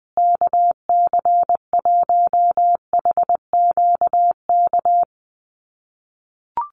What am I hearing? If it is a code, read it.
Morse code, heard as a keyed tone, about 20 wpm, 700 Hz. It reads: KC1HQK
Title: USA Call Signs Repeated 3x 20wpm